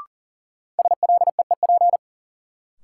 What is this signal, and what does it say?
Morse code, heard as a keyed tone, about 40 wpm, 700 Hz. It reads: SLEEP
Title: Sentences from Top 500 Words - New Word Drill - 40wpm